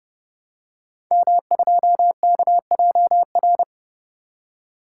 Morse code, heard as a keyed tone, about 30 wpm, 700 Hz. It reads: M2KJR
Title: Code Groups - 5 Letter:Numbers - 30wpm